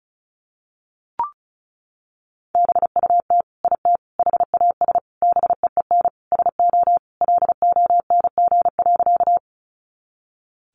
Morse code, heard as a keyed tone, about 34 wpm, 700 Hz. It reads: BUT IT HAS BEEN SO LONG.
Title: Sentences from Top 100 Words 35wpm